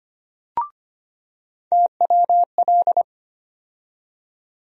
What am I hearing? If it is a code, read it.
Morse code, heard as a keyed tone, about 25 wpm, 700 Hz. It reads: TWL
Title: USA Call Sign Postfixes - 3 Letters 25wpm